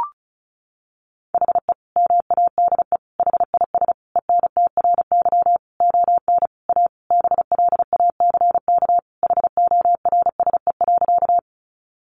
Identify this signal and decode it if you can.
Morse code, heard as a keyed tone, about 34 wpm, 700 Hz. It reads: HE MADE HIS ENTRY ON A BLACK HORSE.
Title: Sentences from Top 1500 Words 35wpm